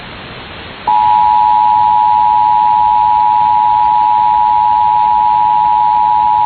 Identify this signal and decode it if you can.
Emergency Alert System (SAME) tones (the 853 and 960 Hz attention tone). It is heard from 1.2 to 6.2 s.